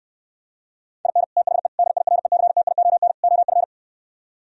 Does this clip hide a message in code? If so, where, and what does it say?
Yes: Morse code from 1.1 to 3.6 s, heard as a keyed tone, about 68 wpm, 700 Hz. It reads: IN THE DIRECTION OF